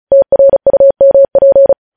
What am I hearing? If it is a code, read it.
Morse code, heard as a keyed tone, about 36 wpm, 550 Hz. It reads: TRUMP
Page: news headlines in morse code at 35 wpm